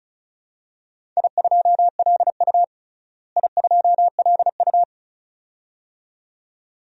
Morse code, heard as a keyed tone, about 36 wpm, 700 Hz. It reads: I2LU I2LU
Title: DXCC-100 Call Signs - 4 Characters - Repeated 2x - 35wpm